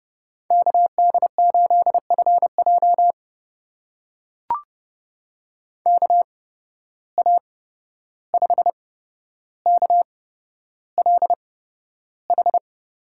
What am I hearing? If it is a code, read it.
Morse code, heard as a keyed tone, about 30 wpm, spaced as for 12 wpm, 700 Hz. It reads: KD8FJ KA5KLH
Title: USA Call Signs - 8x Character Spacing - 30wpm